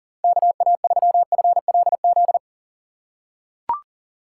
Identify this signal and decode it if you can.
Morse code, heard as a keyed tone, about 40 wpm, 700 Hz. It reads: KA3FLZ
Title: USA Call Signs - 6 Characters - 3x Character Spacing - 40wpm